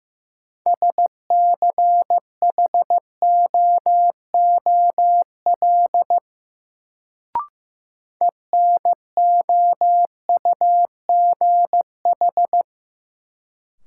Morse code, heard as a keyed tone, about 16 wpm, 700 Hz. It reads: SCHOOL ENOUGH
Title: Sets of 2 Words - New Word Drill - 15wpm